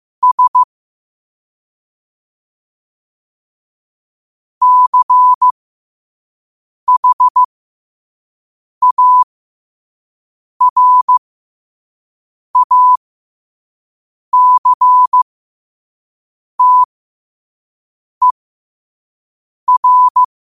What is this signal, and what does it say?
Morse code, heard as a keyed tone, about 15 wpm, spaced as for 5 wpm, 1000 Hz. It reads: S CHARACTER
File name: qotdfiles / 5wpm